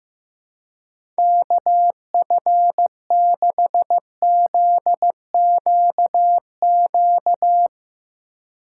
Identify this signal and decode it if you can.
Morse code, heard as a keyed tone, about 15 wpm, 700 Hz. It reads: KF6ZQQ
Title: Speed Racing - USA 2x3 Call Signs 15wpm